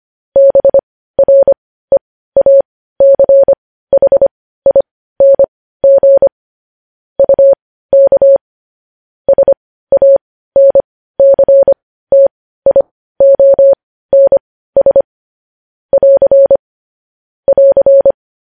Morse code, heard as a keyed tone, about 25 wpm, spaced as for 15 wpm, 550 Hz. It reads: BREACHING UK SANCTIONS <AR> <AR>
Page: news-headlines-in-morse-code-at-15-wpm